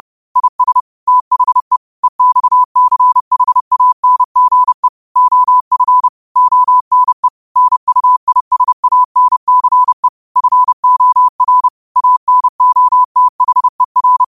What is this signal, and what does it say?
Morse code, heard as a keyed tone, about 30 wpm, 1000 Hz. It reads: IS THE EXCHANGE OF ONE NUISANCE FOR ANOTHER
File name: qotdfiles / 30wpm